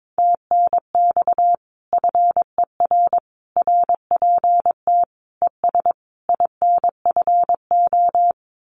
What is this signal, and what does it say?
Morse code, heard as a keyed tone, about 22 wpm, 700 Hz. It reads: TNX FER RPT ES INFO
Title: Common QSO Phrases Encore 22wpm